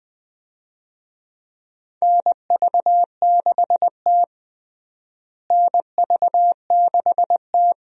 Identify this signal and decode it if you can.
Morse code, heard as a keyed tone, about 20 wpm, 700 Hz. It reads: NV6T NV6T
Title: USA Call Signs - 4 Characters - Repeated 2x - 20wpm